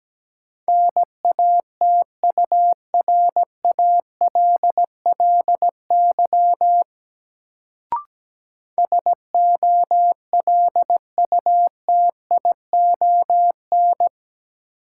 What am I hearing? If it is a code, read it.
Morse code, heard as a keyed tone, about 17 wpm, 700 Hz. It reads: NATURALLY SOLUTION
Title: Sentences from Top 1300 Words - New Word Drill - 17wpm